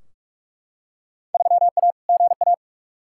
Morse code, heard as a keyed tone, about 45 wpm, 700 Hz. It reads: 3A GA
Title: Field Day Exchanges - 45wpm